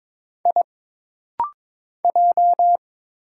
Morse code, heard as a keyed tone, about 22 wpm, 700 Hz. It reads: I J
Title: Single Letters - Rapid-Fire 22wpm